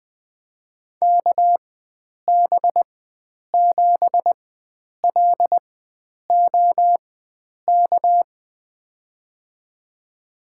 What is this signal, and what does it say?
Morse code, heard as a keyed tone, about 20 wpm, spaced as for 9 wpm, 700 Hz. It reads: KB7LOK
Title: USA Call Signs - 6 Characters - 4x Character Spacing - 20wpm